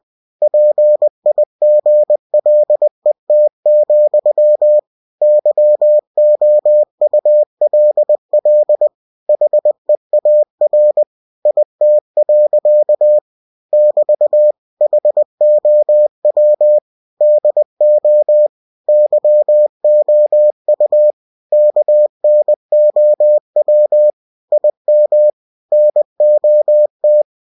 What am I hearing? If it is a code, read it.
Morse code, heard as a keyed tone, about 20 wpm, 600 Hz. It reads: PIGLET, YOULL HEAR IT. <BT> HOW DO YOU KNOW IM NOT
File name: cw-books / winnie-the-pooh / winnie-20wpm-page-0081